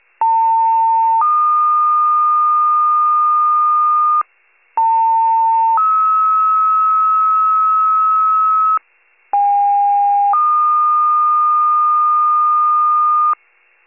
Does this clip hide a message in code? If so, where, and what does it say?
Yes: two-tone page from 0.2 to 13.3 s, heard as a one-second tone, then a three-second tone, about 900 then 1220 Hz, about 900 then 1290 Hz, about 810 then 1220 Hz.